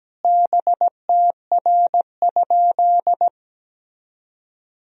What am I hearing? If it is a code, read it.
Morse code, heard as a keyed tone, about 17 wpm, 700 Hz. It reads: BTR?
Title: Sentences from Top 400 Words and Abbreviations 17wpm